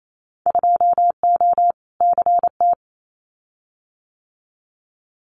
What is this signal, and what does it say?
Morse code, heard as a keyed tone, about 28 wpm, 700 Hz. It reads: 2O CT
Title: Winter Field Day Exchanges 28wpm